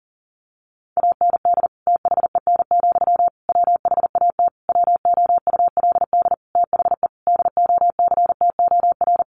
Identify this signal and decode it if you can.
Morse code, heard as a keyed tone, about 40 wpm, 700 Hz. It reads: AND THEN, WHAT WOULD THE DOCTOR
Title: Sentences from Top 1500 Words 40wpm